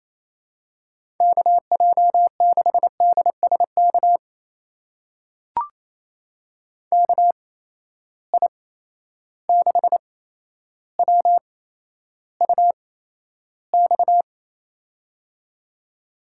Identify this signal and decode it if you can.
Morse code, heard as a keyed tone, about 28 wpm, spaced as for 12 wpm, 700 Hz. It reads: KJ6DSK KI6WUX
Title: USA Call Signs - 6 Characters - 8x Character Spacing - 28wpm